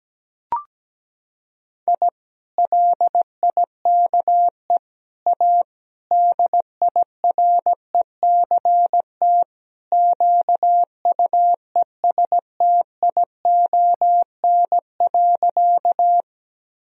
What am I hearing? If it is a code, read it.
Morse code, heard as a keyed tone, about 17 wpm, 700 Hz. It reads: I LIKE A DIRECT QUESTION.
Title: Sentences from Top 1000 Words - Review Part 2 - 17wpm